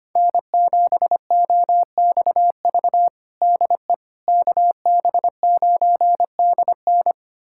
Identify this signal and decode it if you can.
Morse code, heard as a keyed tone, about 25 wpm, 700 Hz. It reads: N7OXV DE KB9DN
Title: USA Callsign DE Callsign - 5 Characters - 25wpm